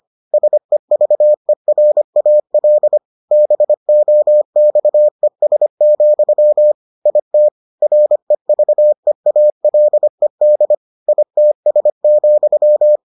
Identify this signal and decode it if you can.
Morse code, heard as a keyed tone, about 25 wpm, 600 Hz. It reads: SEVERAL BOXES, IT REVEALED ITS,